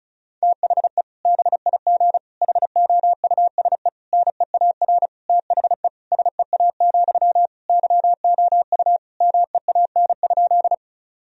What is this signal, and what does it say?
Morse code, heard as a keyed tone, about 36 wpm, 700 Hz. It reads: THE BIG HOUSE NEAR THE SEA, YOU MEAN?
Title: Sentences from Top 1000 Words - Review Part 1 - 35wpm